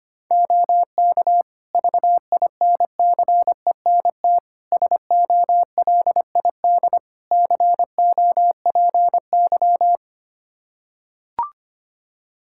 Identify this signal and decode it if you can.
Morse code, heard as a keyed tone, about 25 wpm, 700 Hz. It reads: OK VINCENT SOLID COPY
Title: Practice QSO - Exchange 3 - More Info - 25wpm